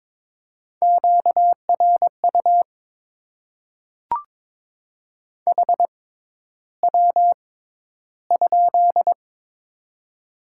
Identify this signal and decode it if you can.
Morse code, heard as a keyed tone, about 22 wpm, spaced as for 11 wpm, 700 Hz. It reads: QRU HW?
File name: Top 90 QSO Elements - 6x Character Spacing - 22wpm